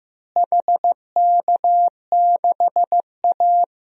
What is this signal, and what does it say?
Morse code, heard as a keyed tone, about 16 wpm, 700 Hz. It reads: HK6A
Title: DXCC-100 Call Signs - 4 Characters - Repeated 2x - 15wpm